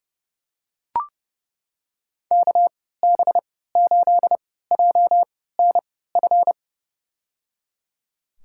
Morse code, heard as a keyed tone, about 31 wpm, spaced as for 17 wpm, 700 Hz. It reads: KB8JNF
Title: USA Call Signs - 6 Characters - 3x Character Spacing - 30wpm